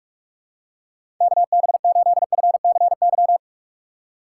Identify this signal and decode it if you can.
Morse code, heard as a keyed tone, about 44 wpm, 700 Hz. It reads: KB8FCY
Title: USA Call Signs - 6 Characters - 6x Character Spacing - 45wpm